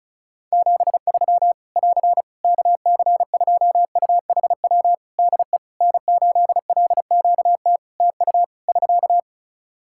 Morse code, heard as a keyed tone, about 36 wpm, 700 Hz. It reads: 73 <AR> KC2UHW DE N8LQT TU <SK>
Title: Practice QSO - Exchange 4 - Ending - 35wpm